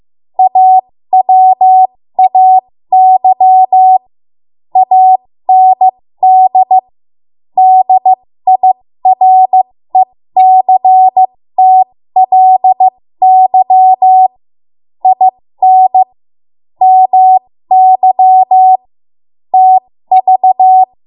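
Morse code, heard as a keyed tone, about 15 wpm, spaced as for 13 wpm, 750 Hz. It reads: AWAY AND DIRECTLY IN MY TV
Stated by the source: ARRL Morse code practice at 13 WPM.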